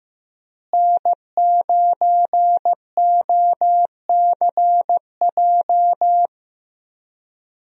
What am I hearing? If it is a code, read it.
Morse code, heard as a keyed tone, about 15 wpm, 700 Hz. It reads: N9OCJ
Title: USA Call Signs - 5 Characters - Repeated 3x - 15wpm